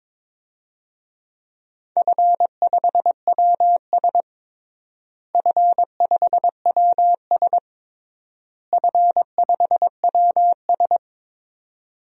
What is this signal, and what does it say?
Morse code, heard as a keyed tone, about 22 wpm, 700 Hz. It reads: F5WS F5WS F5WS
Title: DXCC-100 Call Signs - 4 Characters - Repeated 3x - 22wpm